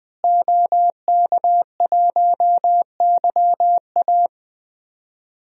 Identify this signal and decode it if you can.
Morse code, heard as a keyed tone, about 20 wpm, 700 Hz. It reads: OK1YA
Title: DXCC-100 Call Signs - 5 Characters - Repeated 2x - 20wpm